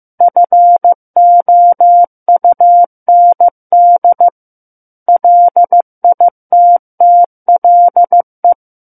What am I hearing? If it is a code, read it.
Morse code, heard as a keyed tone, about 15 wpm, 700 Hz. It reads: FOUND LITTLE
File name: Book One / 06-Chapter 6 - 15 WPM